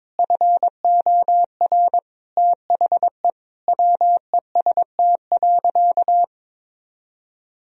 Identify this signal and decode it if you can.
Morse code, heard as a keyed tone, about 22 wpm, 700 Hz. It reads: FOR THE WEST.
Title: Sentences from Top 1000 Words - Review Part 2 - 22wpm